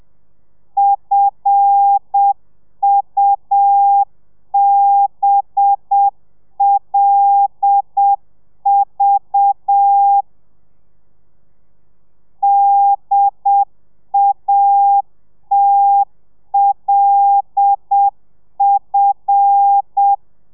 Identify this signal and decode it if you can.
Morse code, heard as a keyed tone, about 7 wpm, 800 Hz. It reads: FUBLV DATLF
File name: telegrafia / i2tfj / Lez5 30